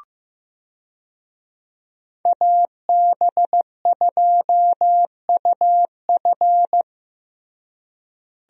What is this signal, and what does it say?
Morse code, heard as a keyed tone, about 15 wpm, 700 Hz. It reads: AB2UF
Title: USA Call Signs - 5 Characters - Repeated 2x - 15wpm